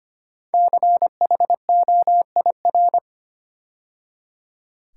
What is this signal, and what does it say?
Morse code, heard as a keyed tone, about 25 wpm, 700 Hz. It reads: CHOIR
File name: 5 Letter Words - First 3 Letters Given - 25wpm